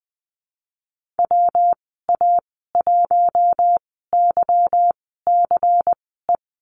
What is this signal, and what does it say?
Morse code, heard as a keyed tone, about 20 wpm, spaced as for 15 wpm, 700 Hz. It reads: WA1YCE
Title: 186 Lesson 30g - Letter b - Call Signs 20wpm